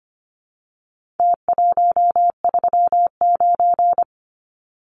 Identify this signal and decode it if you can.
Morse code, heard as a keyed tone, about 25 wpm, 700 Hz. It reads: T139
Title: Numbers - Contest Style - 4-Digit 25wpm